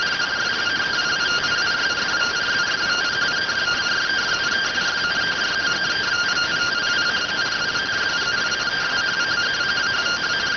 Two-tone teleprinter signal, about 170 Hz shift.